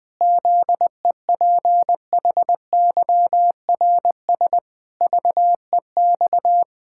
Morse code, heard as a keyed tone, about 20 wpm, 700 Hz. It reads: ZEPHYRS VEX
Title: Pangram Challenge v2 - 20wpm